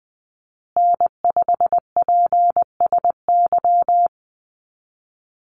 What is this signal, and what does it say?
Morse code, heard as a keyed tone, about 20 wpm, 700 Hz. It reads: N5PSY
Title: USA Call Signs Repeated 2x 20wpm